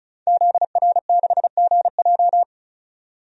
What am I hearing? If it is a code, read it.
Morse code, heard as a keyed tone, about 36 wpm, 700 Hz. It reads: ZR6GJ
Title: DXCC-100 Call Signs - 5 Characters - 35wpm